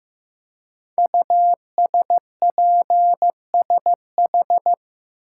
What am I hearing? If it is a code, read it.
Morse code, heard as a keyed tone, about 15 wpm, 700 Hz. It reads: USPSH
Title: Code Groups - 5 Letters - 15wpm